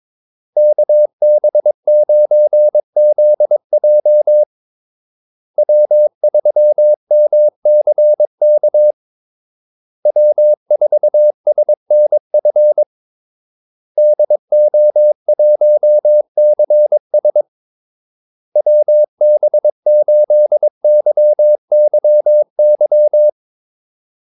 Morse code, heard as a keyed tone, about 22 wpm, 600 Hz. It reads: KB9ZJ W3MCK W4SNF DO1CS WB8YYY
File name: Callsigns-22wpm